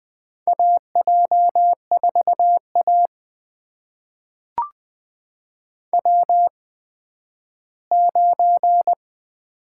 Morse code, heard as a keyed tone, about 20 wpm, spaced as for 12 wpm, 700 Hz. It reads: AJ4A W9
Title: USA Call Signs - 4 Characters - 8x Character Spacing - 20wpm